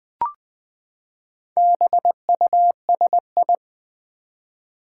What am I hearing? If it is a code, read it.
Morse code, heard as a keyed tone, about 20 wpm, 700 Hz. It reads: BUSI
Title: Build A Word - 4 Letters At A Time - 11-Letter Words - 20wpm